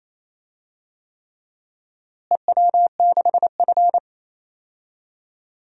Morse code, heard as a keyed tone, about 28 wpm, 700 Hz. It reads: EW6F
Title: DXCC-100 Call Signs - 4 Characters - Repeated 3x - 28wpm